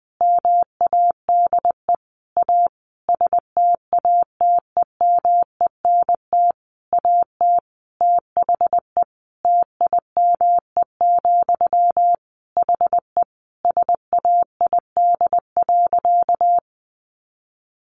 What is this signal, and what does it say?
Morse code, heard as a keyed tone, about 20 wpm, 700 Hz. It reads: MADE A STATEMENT AT THE TIME, HE SAID.
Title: Sentences from Top 1200 Words 20wpm